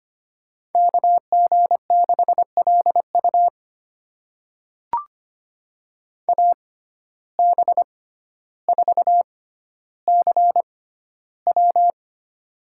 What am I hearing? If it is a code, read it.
Morse code, heard as a keyed tone, about 25 wpm, spaced as for 13 wpm, 700 Hz. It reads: KG6LU AB4CW
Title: USA Call Signs - 5 Characters - 6x Character Spacing - 25wpm